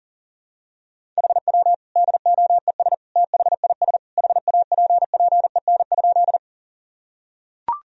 Morse code, heard as a keyed tone, about 40 wpm, 700 Hz. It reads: HW DOES THIS HAPPEN?
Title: Sentences from Top 400 Words and Abbreviations 40wpm